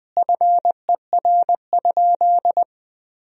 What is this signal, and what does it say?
Morse code, heard as a keyed tone, about 20 wpm, 700 Hz. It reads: FER?
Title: Sentences from Top 200 Words and Abbreviations 20wpm